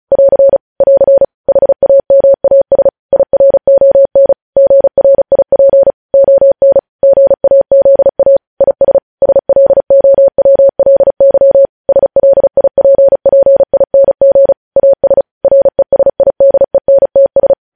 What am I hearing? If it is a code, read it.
Morse code, heard as a keyed tone, about 36 wpm, 550 Hz. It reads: <AR> <AR> HAMAS IRON GRIP ON GAZA IS SLOWLY SLIPPING AS RESIDENTS
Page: news headlines in morse code at 35 wpm